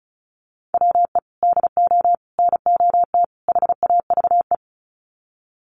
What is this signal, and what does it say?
Morse code, heard as a keyed tone, about 34 wpm, 700 Hz. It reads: WE DO NOT HAVE
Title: Sets of 4 Words 35wpm